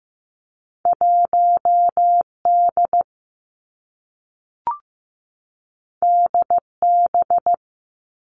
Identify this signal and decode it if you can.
Morse code, heard as a keyed tone, about 15 wpm, 700 Hz. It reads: 1 D D B
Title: Two Challenging Letter-Number Combinations 15wpm